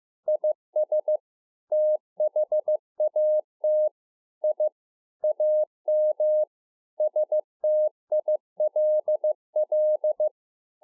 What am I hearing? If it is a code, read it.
Morse code, heard as a keyed tone, about 15 wpm, 610 Hz. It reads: IS THAT I AM STILL
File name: 15WPM